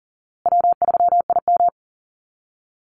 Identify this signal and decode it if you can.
Morse code, heard as a keyed tone, about 40 wpm, 700 Hz. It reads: W3IM
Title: USA Call Signs Repeated 2x 40wpm